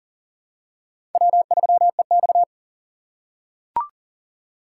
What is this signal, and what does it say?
Morse code, heard as a keyed tone, about 40 wpm, 700 Hz. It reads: W3EX
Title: USA Call Signs - 4 Characters - 4x Character Spacing - 40wpm